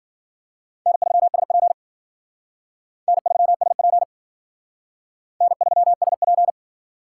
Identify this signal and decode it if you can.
Morse code, heard as a keyed tone, about 50 wpm, 700 Hz. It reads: N3SP N3SP N3SP
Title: Speed Racing - USA Call Signs 40wpm